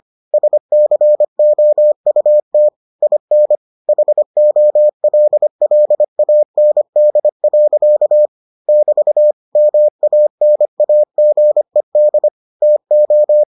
Morse code, heard as a keyed tone, about 25 wpm, 600 Hz. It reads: SCOUT IN HOLLAND. <BT> MANAGED TO